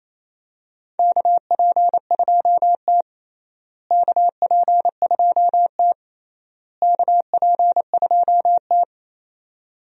Morse code, heard as a keyed tone, about 28 wpm, 700 Hz. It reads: KP2T KP2T KP2T
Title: USA Call Signs - 4 Characters - Repeated 3x - 28wpm